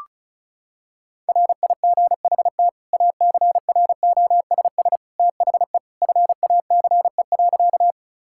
Morse code, heard as a keyed tone, about 36 wpm, 700 Hz. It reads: RIGHT ACROSS THE FACE.
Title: Sentences from Top 1000 Words - Review Part 5 - 35wpm